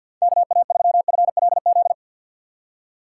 Morse code, heard as a keyed tone, about 50 wpm, 700 Hz. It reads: KA3FLZ
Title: USA Call Signs - 6 Characters - 3x Character Spacing - 50wpm